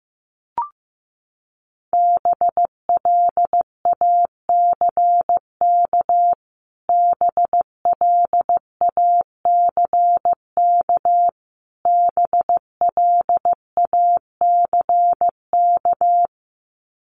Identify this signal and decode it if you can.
Morse code, heard as a keyed tone, about 15 wpm, 700 Hz. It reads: BLACK BLACK BLACK
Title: Top 500 Words Repeated 15wpm